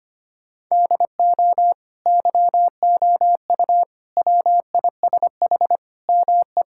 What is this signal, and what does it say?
Morse code, heard as a keyed tone, about 25 wpm, 700 Hz. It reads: DO YOU WISH ME
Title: Sentences from Top 1600 Words 25wpm